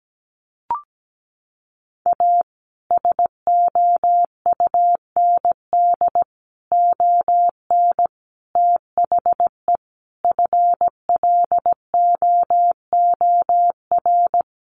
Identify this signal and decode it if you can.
Morse code, heard as a keyed tone, about 17 wpm, 700 Hz. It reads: A SOUND ON THE FLOOR
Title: A Sound Of Thunder 17wpm